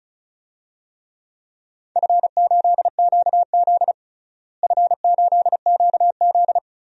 Morse code, heard as a keyed tone, about 34 wpm, 700 Hz. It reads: F8QZ F8QZ
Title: DXCC-100 Call Signs - 4 Characters - Repeated 2x - 35wpm